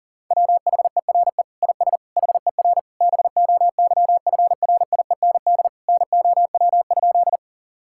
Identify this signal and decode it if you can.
Morse code, heard as a keyed tone, about 40 wpm, 700 Hz. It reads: WHERE IS HER BOYFRIEND NOW?
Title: Sentences from Top 1000 Words - Review Part 5 - 40wpm